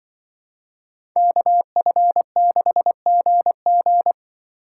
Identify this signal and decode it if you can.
Morse code, heard as a keyed tone, about 24 wpm, 700 Hz. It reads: KF6GG
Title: Speed Racing - USA Call Signs 20wpm